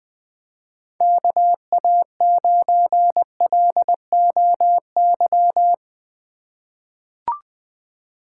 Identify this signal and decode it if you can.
Morse code, heard as a keyed tone, about 20 wpm, 700 Hz. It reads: KA9LOY
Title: USA Call Signs - 6 Characters - 3x Character Spacing - 20wpm